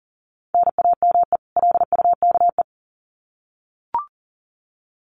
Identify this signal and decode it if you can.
Morse code, heard as a keyed tone, about 40 wpm, 700 Hz. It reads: NAME LUKE
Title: Common QSO Phrases 40wpm